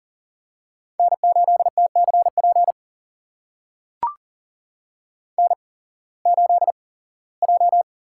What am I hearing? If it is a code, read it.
Morse code, heard as a keyed tone, about 38 wpm, spaced as for 21 wpm, 700 Hz. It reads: N8TCP N8J
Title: USA Call Signs - 5 Characters - 8x Character Spacing - 40wpm